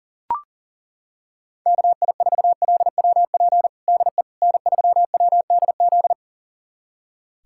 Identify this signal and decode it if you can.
Morse code, heard as a keyed tone, about 37 wpm, 700 Hz. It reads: KI4LWP DE N3WDZ
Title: USA Callsign DE Callsign 40wpm